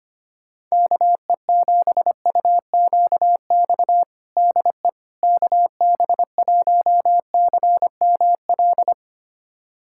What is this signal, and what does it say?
Morse code, heard as a keyed tone, about 25 wpm, 700 Hz. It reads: KE7UQX DE KB1CML
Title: USA Callsign DE Callsign 25wpm